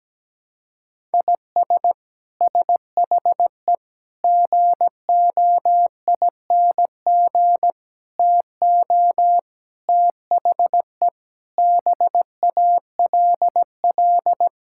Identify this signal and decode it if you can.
Morse code, heard as a keyed tone, about 17 wpm, 700 Hz. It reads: IS SHE GOING TO THE BALL
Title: Sentences from Top 1000 Words - Review Part 6 - 17wpm